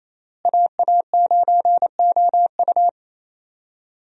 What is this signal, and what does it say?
Morse code, heard as a keyed tone, about 28 wpm, 700 Hz. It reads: AA9OU
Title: USA Call Signs - 5 Characters - Repeated 3x - 28wpm